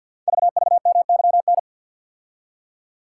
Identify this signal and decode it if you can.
Morse code, heard as a keyed tone, about 52 wpm, 700 Hz. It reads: VVMYN
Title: Code Groups - 5 Letters - 50wpm